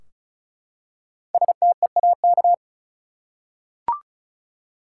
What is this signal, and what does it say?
Morse code, heard as a keyed tone, about 36 wpm, 700 Hz. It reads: STEAK
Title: Five Letter Words - 35wpm